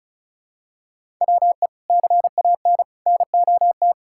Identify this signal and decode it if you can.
Morse code, heard as a keyed tone, about 34 wpm, 700 Hz. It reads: WE CAN NOT
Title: Sentences from Top 1000 Words - Review Part 6 - 35wpm